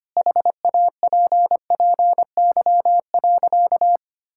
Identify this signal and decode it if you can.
Morse code, heard as a keyed tone, about 25 wpm, 700 Hz. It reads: HAPPY.
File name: practice / 25wpm / Sentences with 2-Word Phrases - 25wpm